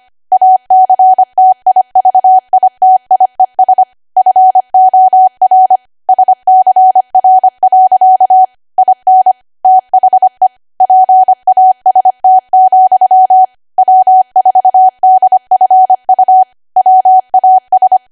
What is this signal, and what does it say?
Morse code, heard as a keyed tone, about 25 wpm, 750 Hz. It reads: ACTIVITIES FOR SCR. IN THE PAST, W4DFU WAS
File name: Archive / 25 WPM / 25WPM